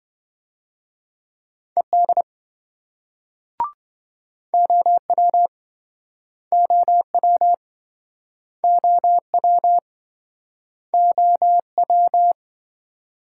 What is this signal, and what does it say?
Morse code, heard as a keyed tone, about 26 wpm, 700 Hz. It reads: ED OW OW OW OW